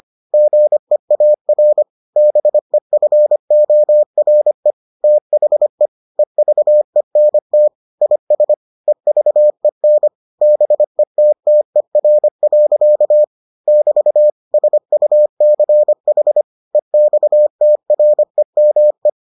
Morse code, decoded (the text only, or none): GEAR BEFORE THE EVENT IS EVEN BETTER. <BT> SUCH EXTREME